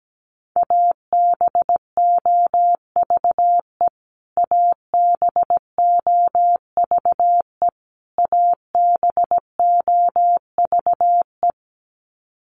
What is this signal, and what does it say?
Morse code, heard as a keyed tone, about 17 wpm, 700 Hz. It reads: ABOVE ABOVE ABOVE
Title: Top 500 Words Repeated 17wpm